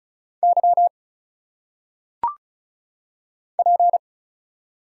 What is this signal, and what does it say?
Morse code, heard as a keyed tone, about 34 wpm, 700 Hz. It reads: Y P
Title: Single Letters - Phonetic Pronunciation 35wpm